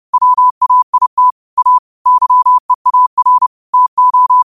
Morse code, heard as a keyed tone, about 30 wpm, 1000 Hz. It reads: WAIT A YEAR TO
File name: qotdfiles / 30wpm